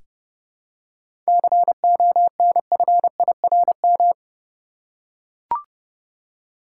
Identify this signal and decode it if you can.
Morse code, heard as a keyed tone, about 31 wpm, 700 Hz. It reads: CONFIRM
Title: Seven Letter Words - 30wpm